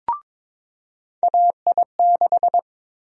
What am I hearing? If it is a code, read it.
Morse code, heard as a keyed tone, about 22 wpm, 700 Hz. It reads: AI6
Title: USA Call Sign Prefixes - 22wpm